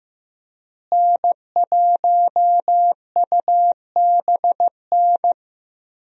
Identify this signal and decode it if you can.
Morse code, heard as a keyed tone, about 15 wpm, 700 Hz. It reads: N1UBN
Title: USA Call Signs - 5 Characters - Repeated 2x - 15wpm